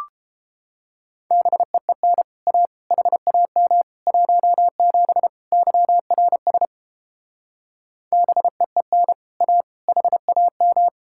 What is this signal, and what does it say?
Morse code, heard as a keyed tone, about 29 wpm, 700 Hz. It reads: BEEN A HAM 17 YRS BEEN A HAM